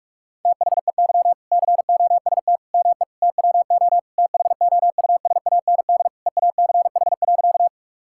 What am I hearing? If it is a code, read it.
Morse code, heard as a keyed tone, about 46 wpm, 700 Hz. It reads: THEY COST ME TWO THOUSAND EACH.
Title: Sentences from Top 1000 Words - Review Part 5 - 45wpm